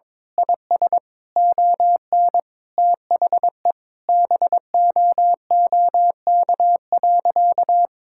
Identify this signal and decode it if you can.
Morse code, heard as a keyed tone, about 22 wpm, 700 Hz. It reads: IS ON THE BOOK.
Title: Sentences from Top 1600 Words 22wpm